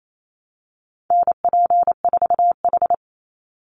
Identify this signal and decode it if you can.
Morse code, heard as a keyed tone, about 28 wpm, 700 Hz. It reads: NP4H
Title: Call Signs Repeated 3x - 28wpm